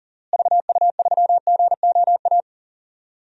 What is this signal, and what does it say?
Morse code, heard as a keyed tone, about 40 wpm, 700 Hz. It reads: VU3GOA
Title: DXCC-100 Call Signs - 6 Characters - Repeated 2x - 40wpm